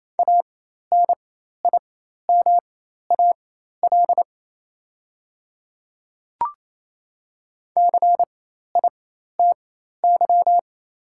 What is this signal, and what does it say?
Morse code, heard as a keyed tone, about 28 wpm, spaced as for 13 wpm, 700 Hz. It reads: ANIMAL CITY
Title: 4x Character Spacing - Top 300 Words - 28wpm